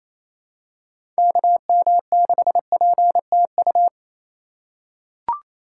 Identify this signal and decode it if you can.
Morse code, heard as a keyed tone, about 28 wpm, 700 Hz. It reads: KM6PTU
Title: USA Call Signs - 6 Characters - 3x Character Spacing - 28wpm